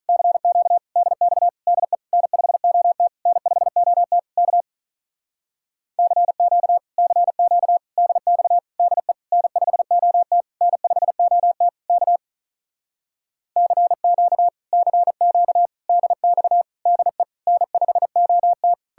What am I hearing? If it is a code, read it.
Morse code, heard as a keyed tone, about 47 wpm, 700 Hz. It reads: CQ DX DE N5OT N5OT K CQ CQ DX DE N5OT N5OT K CQ CQ DX DE N5OT